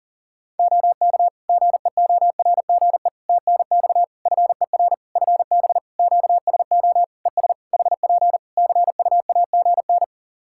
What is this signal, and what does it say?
Morse code, heard as a keyed tone, about 42 wpm, 700 Hz. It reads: OK GEORGE TNX FER FB QSO ES HP CUAGN
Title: K1USN SST QSO Practice 40wpm